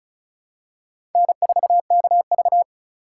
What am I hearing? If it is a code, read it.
Morse code, heard as a keyed tone, about 34 wpm, 700 Hz. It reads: N4KV
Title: USA Call Signs - 4 Characters - 4x Character Spacing - 35wpm